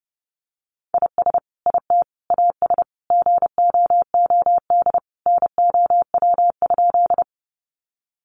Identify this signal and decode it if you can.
Morse code, heard as a keyed tone, about 30 wpm, 700 Hz. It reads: IS IT AS GOOD NOW?
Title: Sentences from Top 200 Words 30wpm